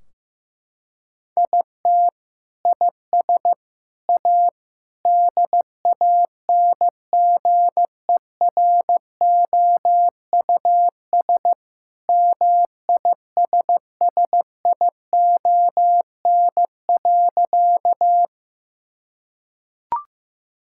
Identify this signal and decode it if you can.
Morse code, heard as a keyed tone, about 15 wpm, 700 Hz. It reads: IT IS A DANGEROUS MISSION.
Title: Sentences from Top 1600 Words 15wpm